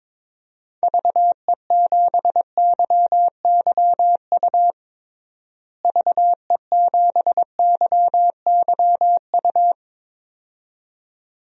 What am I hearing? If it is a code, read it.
Morse code, heard as a keyed tone, about 22 wpm, 700 Hz. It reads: VE7YYU VE7YYU
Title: DXCC-100 Call Signs - 6 Characters - Repeated 2x - 22wpm